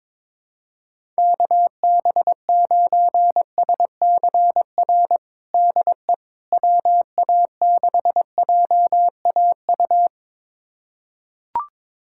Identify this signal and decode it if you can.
Morse code, heard as a keyed tone, about 21 wpm, 700 Hz. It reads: KB9SCR DE WA6JAU
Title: USA Callsign DE Callsign - 6 Characters - 22wpm